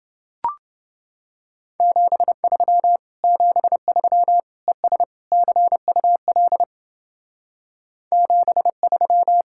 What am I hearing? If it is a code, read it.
Morse code, heard as a keyed tone, about 29 wpm, 700 Hz. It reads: 73 73 ES CUL 73